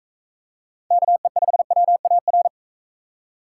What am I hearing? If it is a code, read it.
Morse code, heard as a keyed tone, about 42 wpm, 700 Hz. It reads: KE5WAR